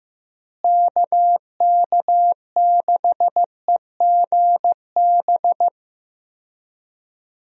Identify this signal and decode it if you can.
Morse code, heard as a keyed tone, about 15 wpm, 700 Hz. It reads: KK6EGB
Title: USA Call Signs - 6 Characters - Repeated 3x - 15wpm